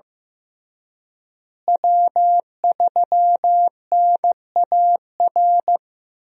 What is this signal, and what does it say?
Morse code, heard as a keyed tone, about 15 wpm, 700 Hz. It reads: W3NAR
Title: USA Call Signs - 5 Characters - Repeated 3x - 15wpm